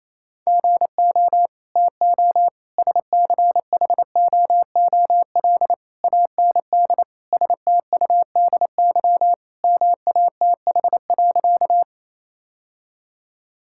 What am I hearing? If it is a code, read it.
Morse code, heard as a keyed tone, about 28 wpm, 700 Hz. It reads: GO TO SCHOOL AND STUDY MATH.
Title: Sentences from Top 1000 Words - Review Part 6 - 28wpm